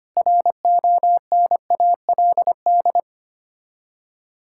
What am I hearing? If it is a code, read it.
Morse code, heard as a keyed tone, about 25 wpm, 700 Hz. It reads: RONALD
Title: Practice QSO - Exchange 1 - Beginning - 25wpm